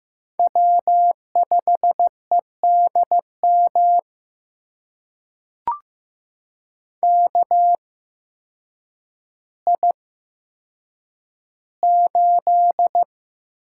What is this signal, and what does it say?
Morse code, heard as a keyed tone, about 15 wpm, spaced as for 8 wpm, 700 Hz. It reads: W5EDM KI8
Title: USA Call Signs - 5 Characters - 8x Character Spacing - 15wpm